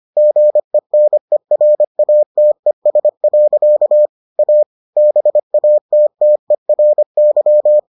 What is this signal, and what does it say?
Morse code, heard as a keyed tone, about 25 wpm, 600 Hz. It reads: GENERATES. A BATTERY